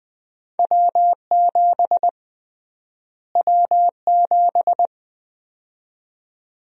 Morse code, heard as a keyed tone, about 20 wpm, 700 Hz. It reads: W7 W7
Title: USA Call Sign Prefixes - Repeated 2x - 20wpm